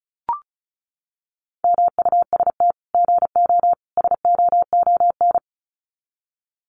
Morse code, heard as a keyed tone, about 34 wpm, 700 Hz. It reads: MUST GO SOON
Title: Common QSO Phrases 35wpm